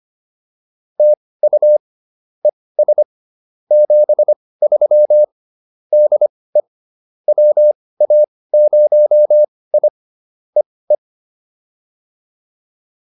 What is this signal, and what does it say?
Morse code, heard as a keyed tone, about 25 wpm, spaced as for 18 wpm, 600 Hz. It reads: TU ES 73 DE WA0I E E